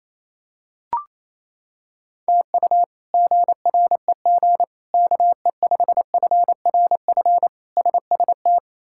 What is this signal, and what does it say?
Morse code, heard as a keyed tone, about 28 wpm, 700 Hz. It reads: TU GREG KE5FRF SST
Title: K1USN SST QSO Practice - 28wpm